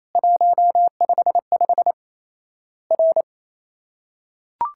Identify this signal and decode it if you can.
Morse code, heard as a keyed tone, about 28 wpm, 700 Hz. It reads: 155 R
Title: Worked All Europe DX Contest - QTC Exchange - 28wpm